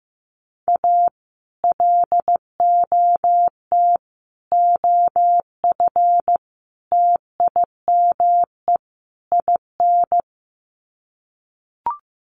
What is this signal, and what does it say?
Morse code, heard as a keyed tone, about 15 wpm, 700 Hz. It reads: A LOT OF TIME IN
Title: Sets of 5 Words Encore2 15wpm